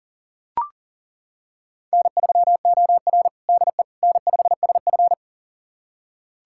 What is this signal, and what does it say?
Morse code, heard as a keyed tone, about 40 wpm, 700 Hz. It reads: N3OR DE N5SF
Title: USA Callsign DE Callsign - 4 Characters - 40wpm